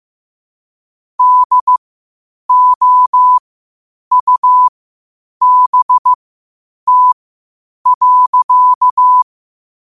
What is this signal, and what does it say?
Morse code, heard as a keyed tone, about 15 wpm, spaced as for 8 wpm, 1000 Hz. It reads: DOUBT.
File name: qotdfiles / 8wpm